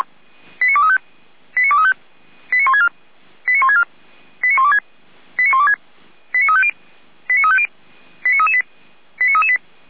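Five-tone selective call (ZVEI), heard as quick runs of five tones, tones of about 70 ms. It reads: 89235 89235 89164 89164 89126 89126 89350 89350 89338 89338
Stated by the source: ZVEI five-tone selective call 89700 received on an SDR.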